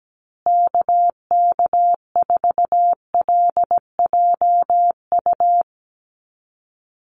Morse code, heard as a keyed tone, about 17 wpm, 700 Hz. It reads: KK4LJU
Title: USA Call Signs Repeated 2x 17wpm